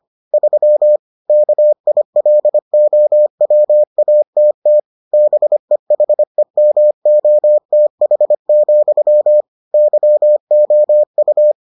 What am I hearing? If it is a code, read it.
Morse code, heard as a keyed tone, about 25 wpm, 600 Hz. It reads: 3 KILOWATT BEHEMOTH, YOU